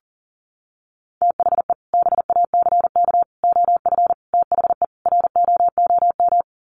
Morse code, heard as a keyed tone, about 40 wpm, 700 Hz. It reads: THE BACK OF THE ROOM
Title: Sets of 5 Words Encore 40wpm